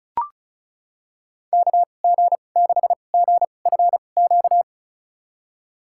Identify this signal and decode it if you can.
Morse code, heard as a keyed tone, about 34 wpm, spaced as for 25 wpm, 700 Hz. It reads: KG6GFQ
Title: USA Call Signs - 6 Characters - 2x Character Spacing - 35wpm